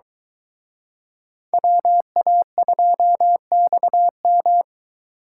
Morse code, heard as a keyed tone, about 24 wpm, 700 Hz. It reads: WA2XM